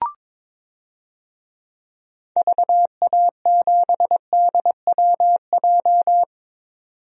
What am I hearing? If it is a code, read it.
Morse code, heard as a keyed tone, about 22 wpm, 700 Hz. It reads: VA7DWJ
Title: DXCC-100 Call Signs - 6 Characters - 22wpm